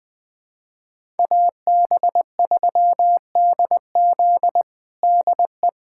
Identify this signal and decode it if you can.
Morse code, heard as a keyed tone, about 21 wpm, 700 Hz. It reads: AB3DZ DE
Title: USA Callsign DE Callsign - 5 Characters - 20wpm